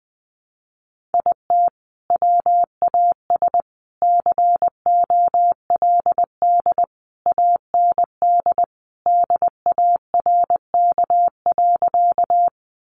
Morse code, heard as a keyed tone, about 20 wpm, 700 Hz. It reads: IT WAS COLD AND DARK.
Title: Sentences from Top 500 Words 20wpm